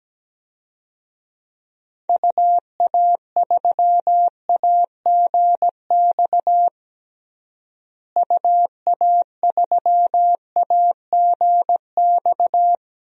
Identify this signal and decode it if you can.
Morse code, heard as a keyed tone, about 17 wpm, 700 Hz. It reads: UA3AGX UA3AGX
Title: DXCC-100 Call Signs - 6 Characters - Repeated 3x - 17wpm